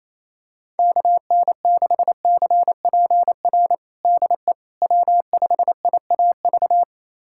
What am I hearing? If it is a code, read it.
Morse code, heard as a keyed tone, about 28 wpm, 700 Hz. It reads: KN6CPR DE W5IAV
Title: USA Callsign DE Callsign 28wpm